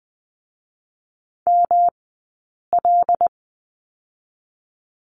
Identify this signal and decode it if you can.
Morse code, heard as a keyed tone, about 20 wpm, 700 Hz. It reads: M L
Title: 106 Lesson 18d - Letter m - Two Characters 20wpm